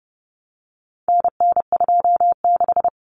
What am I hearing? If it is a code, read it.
Morse code, heard as a keyed tone, about 30 wpm, 700 Hz. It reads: NN26
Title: Common Cut Numbers - 4-Digit 30wpm